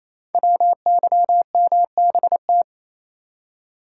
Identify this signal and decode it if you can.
Morse code, heard as a keyed tone, about 27 wpm, 700 Hz. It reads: WYMBT
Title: Code Groups - 5 Letters - 28wpm